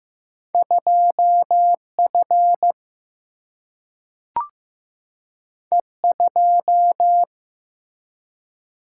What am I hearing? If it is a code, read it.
Morse code, heard as a keyed tone, about 15 wpm, 700 Hz. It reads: 2F E2
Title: Single-Letter Single-Number 15wpm